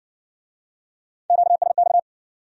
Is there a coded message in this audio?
Yes: Morse code from 1.3 to 2.0 s, heard as a keyed tone, about 54 wpm, 700 Hz. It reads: QSX